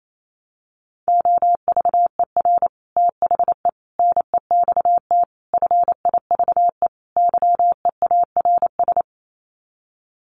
Morse code, heard as a keyed tone, about 28 wpm, 700 Hz. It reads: OVER THE NEXT FIVE YEARS
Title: Sets of 5 Words 28wpm